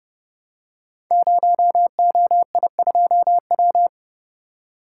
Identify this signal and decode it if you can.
Morse code, heard as a keyed tone, about 28 wpm, 700 Hz. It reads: 0OI2W
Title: Code Groups - 5 Letter:Numbers - 30wpm